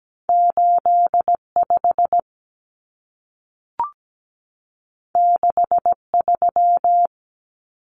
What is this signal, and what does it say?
Morse code, heard as a keyed tone, about 17 wpm, 700 Hz. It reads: 85 63
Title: Common Cut Numbers - 2-Digit 17wpm